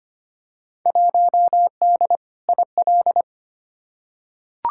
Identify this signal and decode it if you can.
Morse code, heard as a keyed tone, about 25 wpm, 700 Hz. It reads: 1D IL
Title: Field Day Exchanges - 25wpm